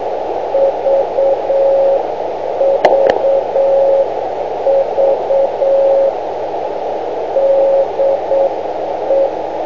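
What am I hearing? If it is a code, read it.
Morse code, heard as a keyed tone, about 8 wpm, 550 Hz, with radio noise. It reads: VVV DE
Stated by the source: Morse code from VX9MRC on 500 kc.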